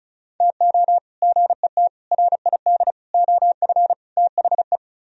Morse code, heard as a keyed tone, about 47 wpm, spaced as for 41 wpm, 700 Hz. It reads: TO GET RID OF THE
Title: Speed-Racing - Sets of 5 Words 35wpm